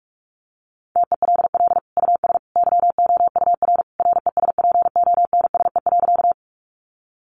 Morse code, heard as a keyed tone, about 44 wpm, 700 Hz. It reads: TELL US YOUR RESPONSE.
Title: Sentences from Top 1400 Words 45wpm